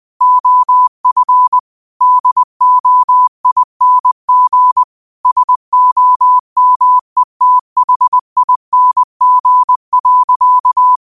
Morse code, heard as a keyed tone, about 20 wpm, 1000 Hz. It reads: OF DOING SOMETHING.